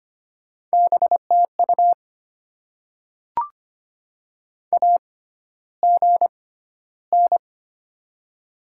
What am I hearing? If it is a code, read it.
Morse code, heard as a keyed tone, about 25 wpm, spaced as for 13 wpm, 700 Hz. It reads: BTU AGN
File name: Top 20 QSO Elements - 6x Character Spacing - 25wpm